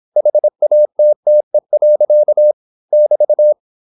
Morse code, heard as a keyed tone, about 26 wpm, 600 Hz. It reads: HATTE. <BT>